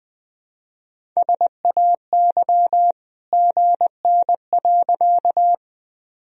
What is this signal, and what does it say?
Morse code, heard as a keyed tone, about 21 wpm, 700 Hz. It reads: SAY GN.
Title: Sentences from Top 300 Words and Abbreviations 20wpm